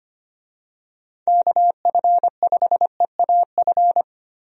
Morse code, heard as a keyed tone, about 25 wpm, 700 Hz. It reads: KF5EAF
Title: USA Call Signs - 6 Characters - Repeated 2x - 25wpm